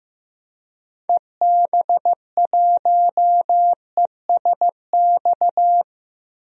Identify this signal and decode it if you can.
Morse code, heard as a keyed tone, about 15 wpm, 700 Hz. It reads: EB1ESX
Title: DXCC-100 Call Signs - 6 Characters - 15wpm